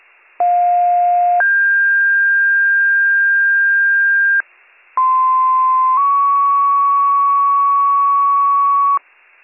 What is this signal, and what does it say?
Two-tone page, heard as a one-second tone, then a three-second tone, about 710 then 1600 Hz, about 1040 then 1120 Hz. It is heard from 0.4 to 9.0 s.